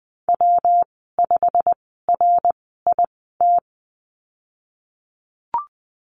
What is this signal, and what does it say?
Morse code, heard as a keyed tone, about 20 wpm, spaced as for 15 wpm, 700 Hz. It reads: W5RIT
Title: 128 Lesson 21g - Number 6 - Call Signs 20wpm